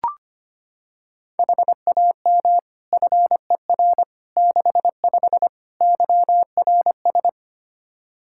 Morse code, heard as a keyed tone, about 25 wpm, 700 Hz. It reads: HAM FER 65 YRS
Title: Practice QSO - Exchange 3 - More Info - 25wpm